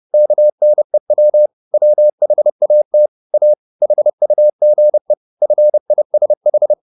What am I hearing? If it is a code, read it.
Morse code, heard as a keyed tone, about 30 wpm, 600 Hz. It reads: KNEW WHAT A HUGE FISH
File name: the old man and the sea / the old man and the sea 1 30wpm